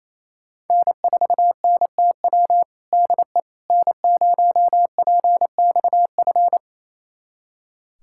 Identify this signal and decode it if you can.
Morse code, heard as a keyed tone, about 28 wpm, 700 Hz. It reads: N4NTW DE N0PXF
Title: USA Callsign DE Callsign - 5 Characters - 28wpm